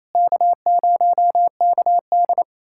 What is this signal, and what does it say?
Morse code, heard as a keyed tone, about 28 wpm, 700 Hz. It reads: K0KD
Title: USA Callsign DE Callsign - 4 Characters - 28wpm